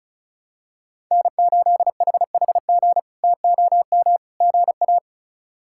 Morse code, heard as a keyed tone, about 36 wpm, 700 Hz. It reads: N8HHG TOM GA
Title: K1USN SST QSO Practice - 35wpm